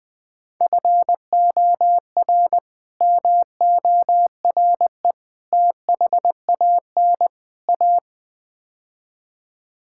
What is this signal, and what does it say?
Morse code, heard as a keyed tone, about 20 wpm, 700 Hz. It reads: FOR MORE THAN A
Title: Build A Phrase - 5-Word Phrases - 20wpm